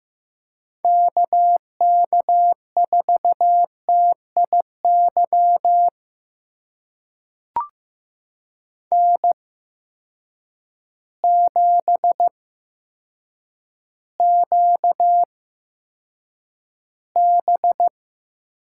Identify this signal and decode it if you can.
Morse code, heard as a keyed tone, about 15 wpm, spaced as for 8 wpm, 700 Hz. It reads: KK4TIY N7QB
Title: USA Call Signs - 8x Character Spacing - 15wpm